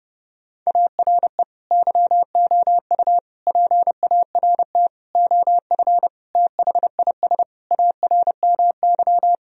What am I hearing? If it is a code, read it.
Morse code, heard as a keyed tone, about 32 wpm, 700 Hz. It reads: ARE YOU PART OF THIS ARMY
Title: Sentences from Top 1000 Words - Review Part 2 - 30wpm